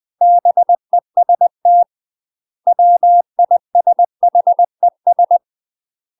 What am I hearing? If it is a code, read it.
Morse code, heard as a keyed tone, about 20 wpm, 700 Hz. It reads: BEST WISHES
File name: UK MASTER QSOs TRAINING / 20WPM / UK MASTER QSOs 0004